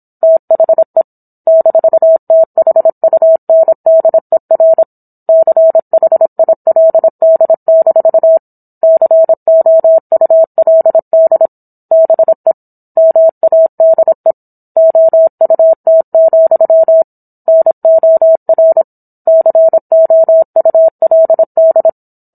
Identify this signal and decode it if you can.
Morse code, heard as a keyed tone, about 27 wpm, 650 Hz. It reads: THE -THUNDER CHILD- COULD BE MADE OUT, NOR COULD